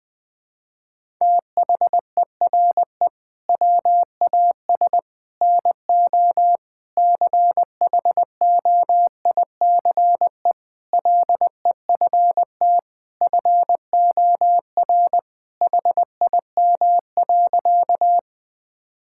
Morse code, heard as a keyed tone, about 21 wpm, 700 Hz. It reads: THERE WAS NO CHOICE LEFT FOR HIM.
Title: Sentences from Top 1000 Words - Review Part 6 - 20wpm